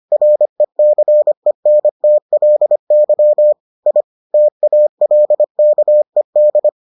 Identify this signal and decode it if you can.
Morse code, heard as a keyed tone, about 25 wpm, 600 Hz. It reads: RECENTLY I TALKED